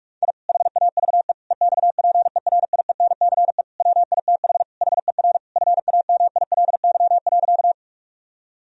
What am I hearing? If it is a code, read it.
Morse code, heard as a keyed tone, about 47 wpm, 700 Hz. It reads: I HAVE EXPERIENCE WITH HER FAMILY.
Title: Sentences from Top 1000 Words - Review Part 3 - 45wpm